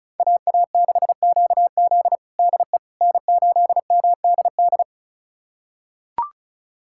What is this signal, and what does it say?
Morse code, heard as a keyed tone, about 34 wpm, 700 Hz. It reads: AA6QZ DE N8MDD
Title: USA Callsign DE Callsign - 5 Characters - 35wpm